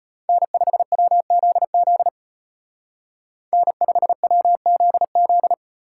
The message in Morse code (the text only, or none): N5WZZ N5WZZ